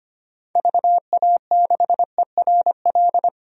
Morse code, heard as a keyed tone, about 25 wpm, 700 Hz. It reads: VA6ERL
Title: DXCC-100 Call Signs - 6 Characters - Repeated 2x - 25wpm